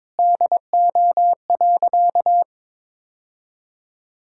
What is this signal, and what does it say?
Morse code, heard as a keyed tone, about 22 wpm, 700 Hz. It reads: DO.
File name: practice / 22wpm / Sentences with 2-Word Phrases - 22wpm